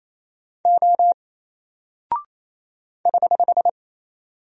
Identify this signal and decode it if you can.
Morse code, heard as a keyed tone, about 28 wpm, 700 Hz. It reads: O <HH>
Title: Single Character (Letter:Number:Punctuation:Prosign) Alt 28wpm